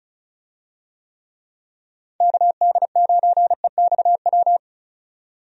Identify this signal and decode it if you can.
Morse code, heard as a keyed tone, about 36 wpm, 700 Hz. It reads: KD9EXW
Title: USA Call Signs - 6 Characters - 35wpm